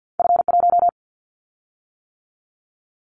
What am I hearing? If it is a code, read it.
Morse code, heard as a keyed tone, about 50 wpm, 700 Hz. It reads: F1
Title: Callsign Prefixes for Top 50 DXCC - 50wpm